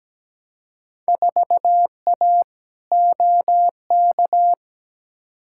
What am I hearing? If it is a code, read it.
Morse code, heard as a keyed tone, about 17 wpm, 700 Hz. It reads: 4A OK
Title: Field Day Exchanges - 17wpm